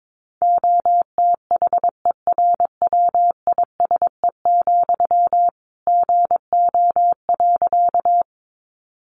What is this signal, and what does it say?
Morse code, heard as a keyed tone, about 22 wpm, 700 Hz. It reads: OTHERWISE, GO.
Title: Sentences from Top 1200 Words 22wpm